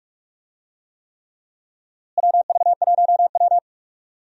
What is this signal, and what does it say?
Morse code, heard as a keyed tone, about 46 wpm, 700 Hz. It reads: WV1W
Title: USA Call Signs - 4 Characters - Repeated 3x - 45wpm